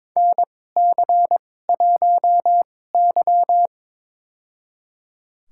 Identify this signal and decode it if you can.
Morse code, heard as a keyed tone, about 22 wpm, spaced as for 16 wpm, 700 Hz. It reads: NC1Y
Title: USA Call Signs - 4 Characters - 2x Character Spacing - 22wpm